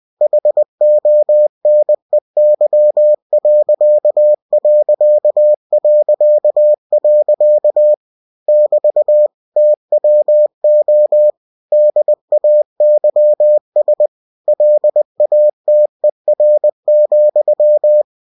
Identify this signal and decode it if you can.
Morse code, heard as a keyed tone, about 20 wpm, 600 Hz. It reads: HONEY.... <BT> TWO DAYS LATER,
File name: cw-books / winnie-the-pooh / winnie-20wpm-page-0094